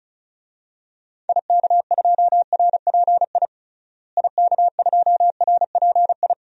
Morse code, heard as a keyed tone, about 36 wpm, 700 Hz. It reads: IK2RPI IK2RPI
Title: DXCC-100 Call Signs - 6 Characters - Repeated 2x - 35wpm